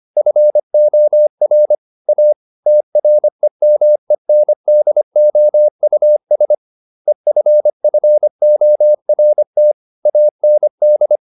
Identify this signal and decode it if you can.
Morse code, heard as a keyed tone, about 25 wpm, 600 Hz. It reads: FOR A TREMENDOUS EFFORT AND